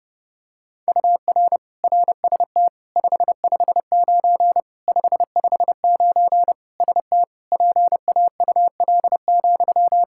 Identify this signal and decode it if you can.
Morse code, heard as a keyed tone, about 30 wpm, 700 Hz. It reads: UR RST 559 559 ST PAUL,
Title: Practice QSO - Exchange 1 - Beginning - 30wpm